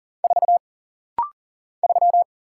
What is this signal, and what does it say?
Morse code, heard as a keyed tone, about 40 wpm, 700 Hz. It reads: 4 3
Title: Single Letter-Number - Rapid-Fire 40wpm